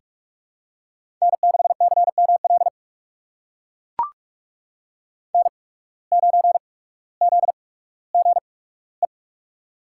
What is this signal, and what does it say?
Morse code, heard as a keyed tone, about 44 wpm, spaced as for 19 wpm, 700 Hz. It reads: N6CML N9ZGE
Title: USA Call Signs - 5 Characters - 8x Character Spacing - 45wpm